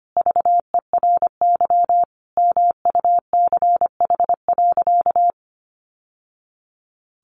Morse code, heard as a keyed tone, about 25 wpm, 700 Hz. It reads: VERY MUCH.
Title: Sentences from Top 1100 Words 25wpm